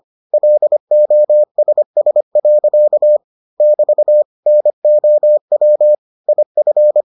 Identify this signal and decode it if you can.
Morse code, heard as a keyed tone, about 25 wpm, 600 Hz. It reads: LOSS. <BT> NOW IF